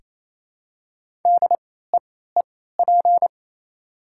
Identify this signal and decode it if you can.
Morse code, heard as a keyed tone, about 28 wpm, spaced as for 16 wpm, 690 Hz. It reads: DEEP
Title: 3x Character Spacing - Top 500 Words - 28wpm